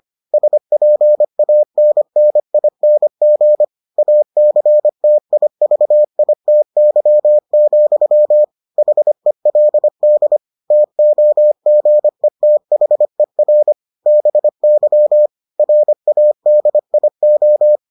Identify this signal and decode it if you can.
Morse code, heard as a keyed tone, about 25 wpm, 600 Hz. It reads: SPANNING ACTIVITY, HELD TOGETHER BY RADIO